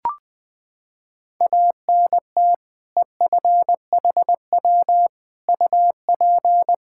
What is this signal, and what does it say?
Morse code, heard as a keyed tone, about 21 wpm, 700 Hz. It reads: ANT EFHW UP
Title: Practice QSO - Exchange 2 - Info - 20wpm